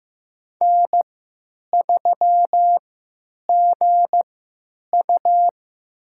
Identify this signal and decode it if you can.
Morse code, heard as a keyed tone, about 15 wpm, spaced as for 9 wpm, 700 Hz. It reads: N3GU
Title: USA Call Signs - 4 Characters - 3x Character Spacing - 15wpm